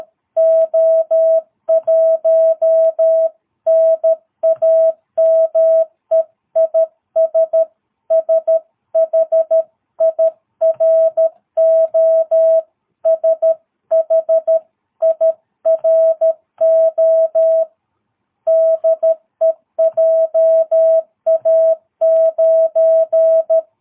Morse code, heard as a keyed tone, about 13 wpm, 650 Hz. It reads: O1 NAME IS SHIRO SHIRO DE JA9
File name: jh8cbh / 2022cw